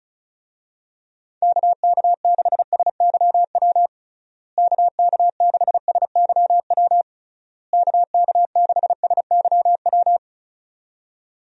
Morse code, heard as a keyed tone, about 36 wpm, 700 Hz. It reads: KK6SYW KK6SYW KK6SYW
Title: USA Call Signs - 6 Characters - Repeated 3x - 35wpm